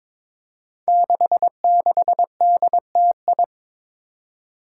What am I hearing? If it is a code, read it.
Morse code, heard as a keyed tone, about 22 wpm, 700 Hz. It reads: 66DTI
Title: Five Challenging Letter-Number Combinations 22wpm